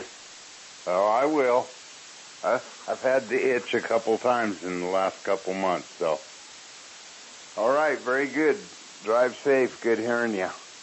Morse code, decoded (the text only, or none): none